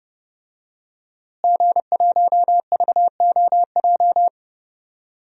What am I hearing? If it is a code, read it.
Morse code, heard as a keyed tone, about 30 wpm, 700 Hz. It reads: G1VOJ
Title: DXCC-100 Call Signs - 5 Characters - Repeated 3x - 30wpm